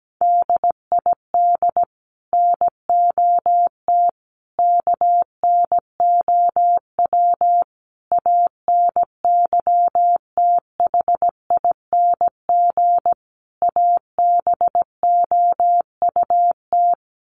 Morse code, heard as a keyed tone, about 17 wpm, 700 Hz. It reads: DID NOT KNOW ANYTHING ABOUT
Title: Sets of 5 Words Encore2 17wpm